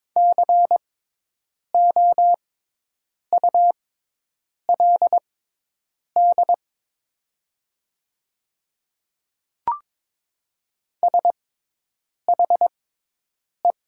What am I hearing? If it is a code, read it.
Morse code, heard as a keyed tone, about 22 wpm, spaced as for 8 wpm, 700 Hz. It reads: COULD SHE
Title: 6x Character Spacing - Top 200 Words - 22wpm